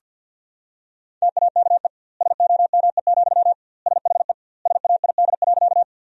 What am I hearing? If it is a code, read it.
Morse code, heard as a keyed tone, about 51 wpm, 700 Hz. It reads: TAKE SOME, SHE SAID.
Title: 2x Word-Spacing - Sentences from Top 200 Words - 50wpm